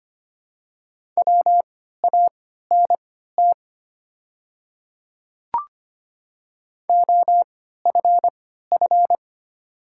Morse code, heard as a keyed tone, about 25 wpm, spaced as for 14 wpm, 690 Hz. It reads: WANT OFF
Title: 3x Character Spacing - Top 200 Words - 25wpm